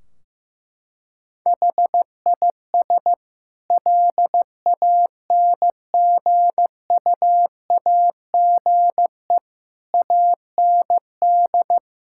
Morse code, heard as a keyed tone, about 16 wpm, 700 Hz. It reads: HIS LANGUAGE AND
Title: Sentences from Top 1600 Words 15wpm